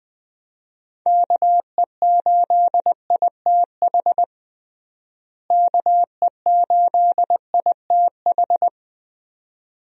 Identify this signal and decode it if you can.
Morse code, heard as a keyed tone, about 20 wpm, 700 Hz. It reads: KE8ITH KE8ITH
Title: USA Call Signs - 6 Characters - Repeated 2x - 20wpm